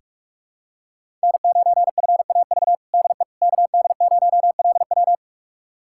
Morse code, heard as a keyed tone, about 46 wpm, 700 Hz. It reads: N9FAV DE KD0LW
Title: USA Callsign DE Callsign - 5 Characters - 45wpm